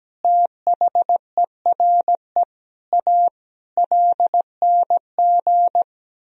Morse code, heard as a keyed tone, about 17 wpm, 700 Hz. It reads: THERE A LNG
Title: Sentences from Top 400 Words and Abbreviations 17wpm